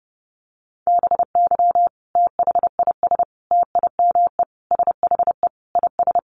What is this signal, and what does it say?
Morse code, heard as a keyed tone, about 30 wpm, 700 Hz. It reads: BY THIS TIME SHE IS
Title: Sentences from Top 700 Words 30wpm